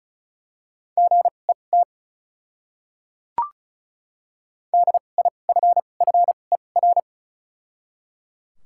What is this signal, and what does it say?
Morse code, heard as a keyed tone, about 34 wpm, spaced as for 25 wpm, 700 Hz. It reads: GET DIFFER
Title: 2x Character Spacing - Top 200 Words - 35wpm